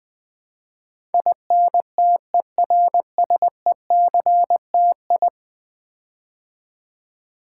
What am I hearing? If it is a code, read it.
Morse code, heard as a keyed tone, about 20 wpm, 700 Hz. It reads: INTERSECTI
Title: Build A Word - 5 Letters At A Time - 12-Letter Words - 20wpm